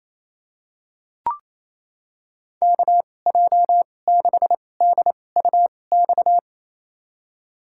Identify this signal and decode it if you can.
Morse code, heard as a keyed tone, about 27 wpm, spaced as for 20 wpm, 700 Hz. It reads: KJ6DUX
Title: USA Call Signs - 6 Characters - 2x Character Spacing - 28wpm